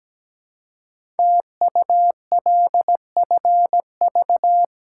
Morse code, heard as a keyed tone, about 17 wpm, 700 Hz. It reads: TULFV
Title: Code Groups - 5 Letters - 17wpm